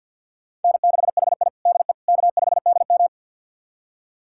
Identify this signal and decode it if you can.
Morse code, heard as a keyed tone, about 48 wpm, 700 Hz. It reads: N6HI DE K5DM
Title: USA Callsign DE Callsign - 4 Characters - 50wpm